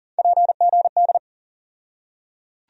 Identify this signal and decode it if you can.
Morse code, heard as a keyed tone, about 40 wpm, 700 Hz. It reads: PGD
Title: USA Call Sign Postfixes - 3 Letters 40wpm